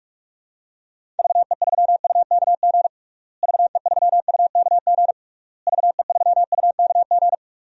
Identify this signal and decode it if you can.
Morse code, heard as a keyed tone, about 46 wpm, 700 Hz. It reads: VE3UKG VE3UKG VE3UKG
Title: DXCC-100 Call Signs - 6 Characters - Repeated 3x - 45wpm